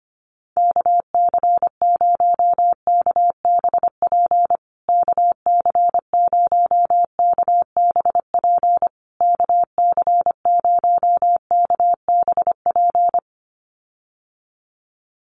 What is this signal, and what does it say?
Morse code, heard as a keyed tone, about 25 wpm, 700 Hz. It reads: KC0KBP KC0KBP KC0KBP
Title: USA Call Signs Repeated 3x 25wpm